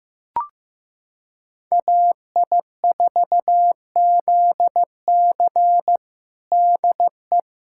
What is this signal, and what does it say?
Morse code, heard as a keyed tone, about 15 wpm, 700 Hz. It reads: AI4ZC DE
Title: USA Callsign DE Callsign - 5 Characters - 15wpm